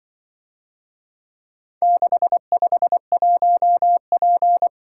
Morse code, heard as a keyed tone, about 24 wpm, 700 Hz. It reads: 651P
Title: Speed-Racing - Four Letter-Number Combinations - 20wpm